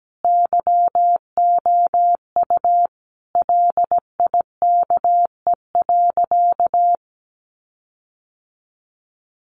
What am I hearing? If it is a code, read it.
Morse code, heard as a keyed tone, about 17 wpm, 700 Hz. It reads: YOU LIKE.
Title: Sentences from Top 1500 Words 17wpm